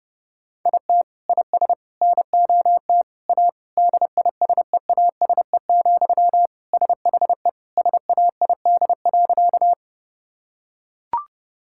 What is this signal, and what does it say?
Morse code, heard as a keyed tone, about 30 wpm, 700 Hz. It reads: IT IS NOT A DISEASE, SHE SAID.
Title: Sentences from Top 1000 Words - Review Part 6 - 30wpm